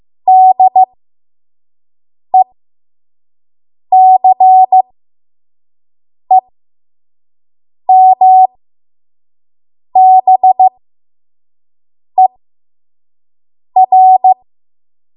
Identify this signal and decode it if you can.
Morse code, heard as a keyed tone, about 15 wpm, spaced as for 5 wpm, 750 Hz. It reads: DECEMBER
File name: Morse / 05WPM